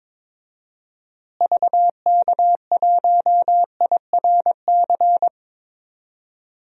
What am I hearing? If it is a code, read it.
Morse code, heard as a keyed tone, about 22 wpm, 700 Hz. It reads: VK1IRC
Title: DXCC-100 Call Signs - 6 Characters - Repeated 2x - 22wpm